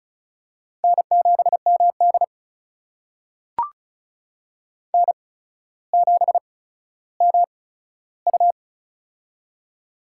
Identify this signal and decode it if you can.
Morse code, heard as a keyed tone, about 36 wpm, spaced as for 15 wpm, 700 Hz. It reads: N7MD N7MU
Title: USA Call Signs - 4 Characters - 8x Character Spacing - 35wpm